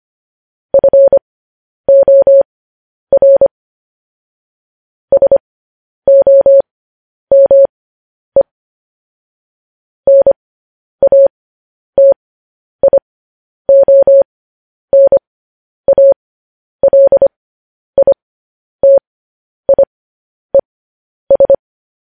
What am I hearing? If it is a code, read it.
Morse code, heard as a keyed tone, about 25 wpm, spaced as for 10 wpm, 550 Hz. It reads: FOR SOME NATIONALITIES
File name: podcast / News headlines-1x-25x10